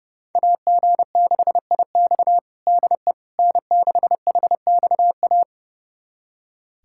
Morse code, heard as a keyed tone, about 32 wpm, 700 Hz. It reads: AG6IX DE N6HXA
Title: USA Callsign DE Callsign - 5 Characters - 30wpm